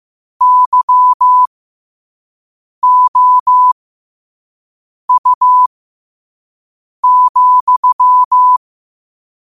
Morse code, heard as a keyed tone, about 15 wpm, spaced as for 5 wpm, 1000 Hz. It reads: YOU,
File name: qotdfiles / 5wpm